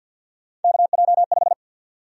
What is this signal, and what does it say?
Morse code, heard as a keyed tone, about 52 wpm, 700 Hz. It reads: KJ5
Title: USA Call Sign Prefixes - 50wpm